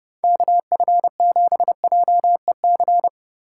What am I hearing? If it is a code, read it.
Morse code, heard as a keyed tone, about 29 wpm, 700 Hz. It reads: KF7JEC